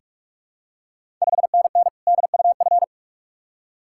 Morse code, heard as a keyed tone, about 46 wpm, 700 Hz. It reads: 5NN DUF
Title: Ontario QSO Party Exchange 45wpm